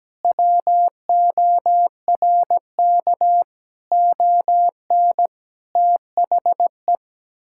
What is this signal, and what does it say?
Morse code, heard as a keyed tone, about 17 wpm, 700 Hz. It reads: WORK ON THE
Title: Sentences from Top 1000 Words - Review Part 5 - 17wpm